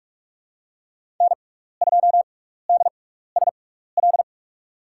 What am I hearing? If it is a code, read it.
Morse code, heard as a keyed tone, about 44 wpm, spaced as for 15 wpm, 700 Hz. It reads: N2DSL
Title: USA Call Signs - 5 Characters - 6x Character Spacing - 45wpm